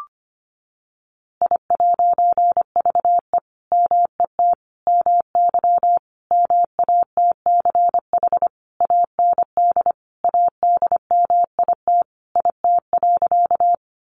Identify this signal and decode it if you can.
Morse code, heard as a keyed tone, about 25 wpm, 700 Hz. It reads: I'VE MET MY MATCH AND ADMIT IT.
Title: A Sound Of Thunder 25wpm